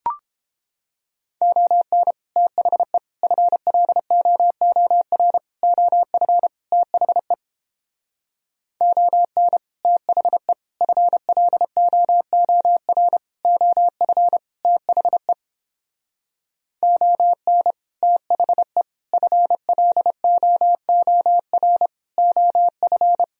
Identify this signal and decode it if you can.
Morse code, heard as a keyed tone, about 29 wpm, 700 Hz. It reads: ON THE FLOOR OF THE ON THE FLOOR OF THE ON THE FLOOR OF